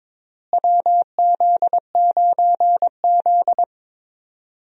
Morse code, heard as a keyed tone, about 22 wpm, 700 Hz. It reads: WZ9Z
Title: USA Call Signs - 4 Characters - Repeated 3x - 22wpm